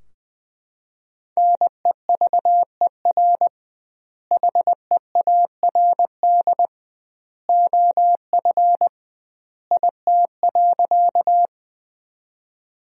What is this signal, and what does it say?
Morse code, heard as a keyed tone, about 20 wpm, 700 Hz. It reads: NEVER HEARD OF IT.
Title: 2x Word-Spacing - Sentences from Top 400 Words - 20wpm